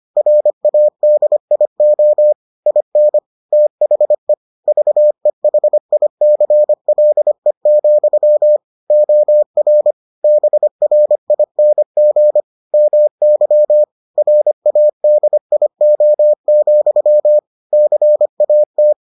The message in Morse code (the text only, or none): RADIO IN THE VEHICLE, OR BRING MY RADIO, CAT